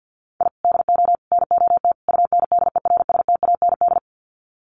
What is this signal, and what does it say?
Morse code, heard as a keyed tone, about 50 wpm, 700 Hz. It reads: I DO NOT UNDERSTAND
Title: Sets of 4 Words 50wpm